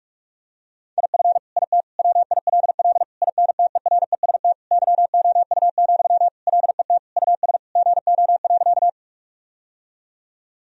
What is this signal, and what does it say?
Morse code, heard as a keyed tone, about 46 wpm, 700 Hz. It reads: IF IT WILL INTEREST YOU, LET US GO.
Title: Sentences from Top 1000 Words - Review Part 2 - 45wpm